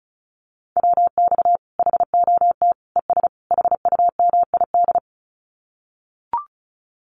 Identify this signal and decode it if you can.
Morse code, heard as a keyed tone, about 34 wpm, 700 Hz. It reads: WX HOT ES HUMID
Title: Common QSO Phrases 35wpm